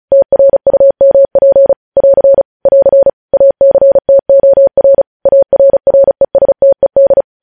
Morse code, heard as a keyed tone, about 36 wpm, 550 Hz. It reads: TRUMP <AR> <AR> ACTOR ARRESTED
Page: news headlines in morse code at 35 wpm